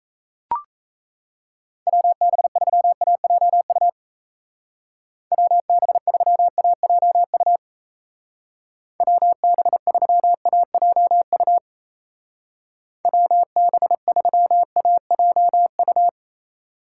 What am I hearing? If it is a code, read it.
Morse code, heard as a keyed tone, about 33 wpm, 700 Hz. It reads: WB3AJU WB3AJU WB3AJU WB3AJU
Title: Speed Racing - USA 2x3 Call Signs 28wpm